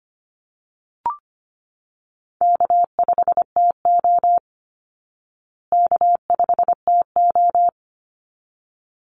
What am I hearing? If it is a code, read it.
Morse code, heard as a keyed tone, about 25 wpm, 700 Hz. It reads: K5TO K5TO
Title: Call Signs Repeated 3x - 25wpm